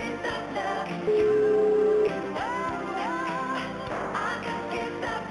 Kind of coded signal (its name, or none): telephone call progress tone